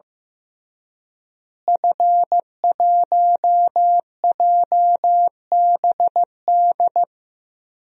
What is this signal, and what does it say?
Morse code, heard as a keyed tone, about 14 wpm, 700 Hz. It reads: F1JBD
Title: DXCC-100 Call Signs - 5 Characters - Repeated 3x - 15wpm